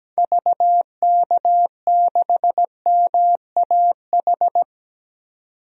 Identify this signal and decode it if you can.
Morse code, heard as a keyed tone, about 17 wpm, 700 Hz. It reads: VK6MAH
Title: DXCC-100 Call Signs - 6 Characters - 17wpm